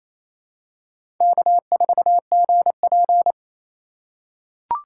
Morse code, heard as a keyed tone, about 28 wpm, 700 Hz. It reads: K4GP
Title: USA Call Signs - 4 Characters - 4x Character Spacing - 28wpm